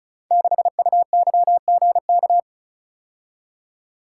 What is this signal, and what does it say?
Morse code, heard as a keyed tone, about 34 wpm, 700 Hz. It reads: 6UYGK
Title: Code Groups - 5 Letter:Numbers - 35wpm